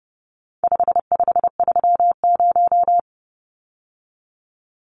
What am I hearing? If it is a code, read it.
Morse code, heard as a keyed tone, about 30 wpm, 700 Hz. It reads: 5530
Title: Numbers - 4-Digit 30wpm